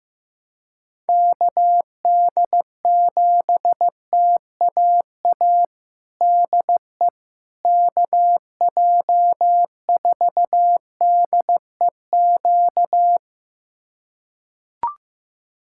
Morse code, heard as a keyed tone, about 15 wpm, 700 Hz. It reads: KD7TAA DE KJ4DEQ
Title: USA Callsign DE Callsign 15wpm